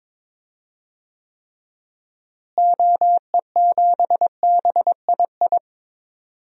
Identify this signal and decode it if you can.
Morse code, heard as a keyed tone, about 22 wpm, 700 Hz. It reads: OE7BII
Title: DXCC-100 Call Signs - 6 Characters - Repeated 3x - 22wpm